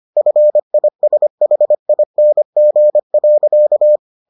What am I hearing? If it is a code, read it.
Morse code, heard as a keyed tone, about 25 wpm, 600 Hz. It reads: FISHING.